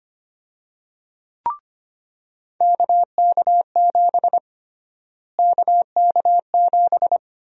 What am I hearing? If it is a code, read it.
Morse code, heard as a keyed tone, about 25 wpm, 700 Hz. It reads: KK7 KK7
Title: USA Call Sign Prefixes - Repeated 2x - 25wpm